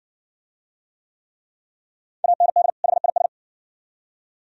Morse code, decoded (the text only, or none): AND HIS